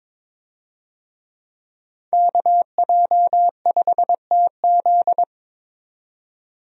Morse code, heard as a keyed tone, about 22 wpm, 700 Hz. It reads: KJ5TZ
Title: USA Call Signs - 5 Characters - 22wpm